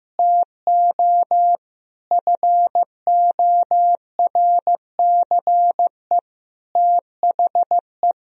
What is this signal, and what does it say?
Morse code, heard as a keyed tone, about 15 wpm, 700 Hz. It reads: TO FORCE THE
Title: Sentences from Top 1000 Words - Review Part 3 - 15wpm